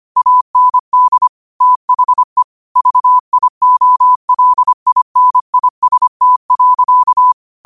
Morse code, heard as a keyed tone, about 25 wpm, 1000 Hz. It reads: AND THE VIOLINIST.